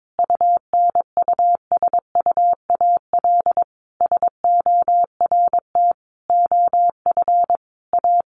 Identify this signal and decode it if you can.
Morse code, heard as a keyed tone, about 22 wpm, 700 Hz. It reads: UNUSUAL SORT OF A
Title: Sentences from Top 1400 Words 22wpm